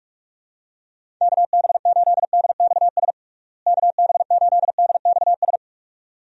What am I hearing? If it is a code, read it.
Morse code, heard as a keyed tone, about 47 wpm, 700 Hz. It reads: KB8DXS KB8DXS
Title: USA Call Signs - 6 Characters - Repeated 2x - 45wpm